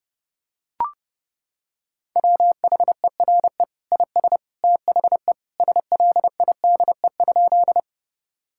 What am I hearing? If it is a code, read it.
Morse code, heard as a keyed tone, about 30 wpm, 700 Hz. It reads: WHERE IS THE SLIDE?
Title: Sentences from Top 1600 Words 30wpm